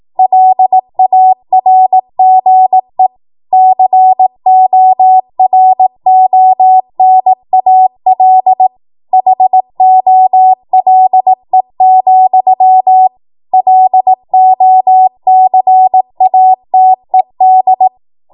Morse code, heard as a keyed tone, about 18 wpm, 750 Hz. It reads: LARGE CORONAL HOLE, LOCATED
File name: Morse / 18WPM